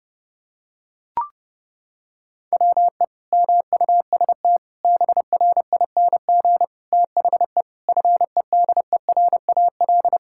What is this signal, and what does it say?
Morse code, heard as a keyed tone, about 32 wpm, 700 Hz. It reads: WE MUST BRING THE FEDERAL
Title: Sentences from Top 1000 Words - Review Part 6 - 30wpm